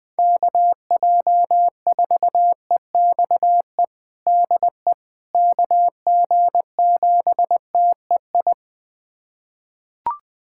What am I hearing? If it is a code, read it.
Morse code, heard as a keyed tone, about 20 wpm, 700 Hz. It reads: KJ4EXE DE KG7TEI
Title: USA Callsign DE Callsign - 6 Characters - 20wpm